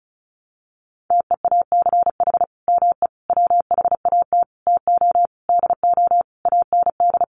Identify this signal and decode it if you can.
Morse code, heard as a keyed tone, about 34 wpm, 700 Hz. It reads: TEACH ME WHAT TO DO AND
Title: Sentences from Top 500 Words 35wpm